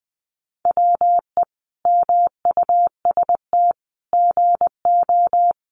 Morse code, heard as a keyed tone, about 20 wpm, 700 Hz. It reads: WE MUST GO
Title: Sentences from Top 300 Words 20wpm